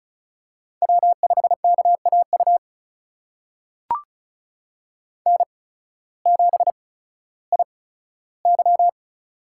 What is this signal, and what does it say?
Morse code, heard as a keyed tone, about 34 wpm, spaced as for 16 wpm, 700 Hz. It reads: W5KAU N7IY
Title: USA Call Signs - 8x Character Spacing - 35wpm